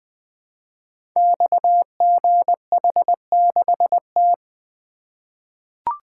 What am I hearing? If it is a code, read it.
Morse code, heard as a keyed tone, about 20 wpm, 700 Hz. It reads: XGH6T
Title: Code Groups - 5 Letter:Numbers - 20wpm